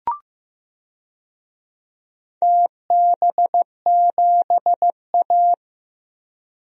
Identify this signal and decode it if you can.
Morse code, heard as a keyed tone, about 15 wpm, 700 Hz. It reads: TB7A
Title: DXCC-100 Call Signs - 4 Characters - 15wpm